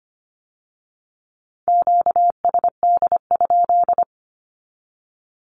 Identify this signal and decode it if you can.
Morse code, heard as a keyed tone, about 25 wpm, 700 Hz. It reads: QSD?
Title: Q-Codes with Spoken Meaning - 25wpm